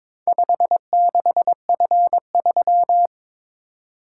Morse code, heard as a keyed tone, about 22 wpm, 700 Hz. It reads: 56F3
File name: practice / 22wpm / Four Letter-Number Combinations - 22wpm